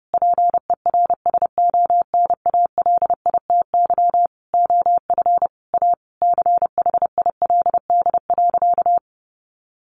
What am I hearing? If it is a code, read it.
Morse code, heard as a keyed tone, about 28 wpm, 700 Hz. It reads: PERSONALITY OF A CHILD.
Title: Sentences from Top 1400 Words 30wpm